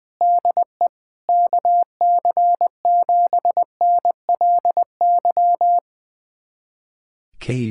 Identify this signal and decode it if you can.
Morse code, heard as a keyed tone, about 21 wpm, 700 Hz. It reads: DE KC7NLY
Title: USA Callsign DE Callsign 20wpm